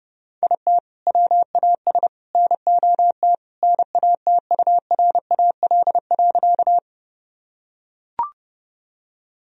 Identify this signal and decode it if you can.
Morse code, heard as a keyed tone, about 30 wpm, 700 Hz. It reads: IT WAS NOT NATURAL.
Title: Sentences from Top 1000 Words - Review Part 4 - 30wpm